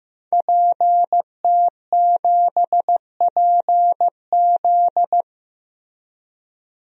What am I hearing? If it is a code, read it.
Morse code, heard as a keyed tone, about 15 wpm, 700 Hz. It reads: PT7PZ
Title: DXCC-100 Call Signs - 5 Characters - Repeated 2x - 15wpm